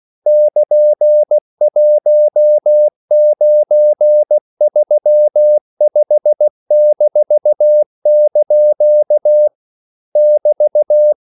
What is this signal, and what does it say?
Morse code, heard as a keyed tone, about 16 wpm, 600 Hz. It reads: <KN>1935-) <BT>